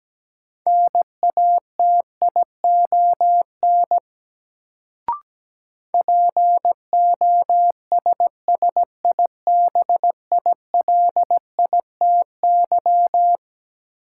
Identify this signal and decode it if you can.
Morse code, heard as a keyed tone, about 17 wpm, 700 Hz. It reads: NATION POSSIBILITY
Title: Sets of 3 Words - New Word Drill - 17wpm